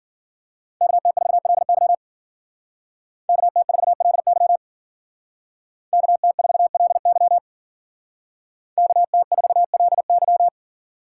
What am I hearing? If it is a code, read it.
Morse code, heard as a keyed tone, about 51 wpm, 700 Hz. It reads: KT4LY KT4LY KT4LY KT4LY